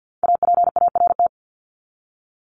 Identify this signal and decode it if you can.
Morse code, heard as a keyed tone, about 50 wpm, 700 Hz. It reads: APART
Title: Top 1501-1750 Words 50wpm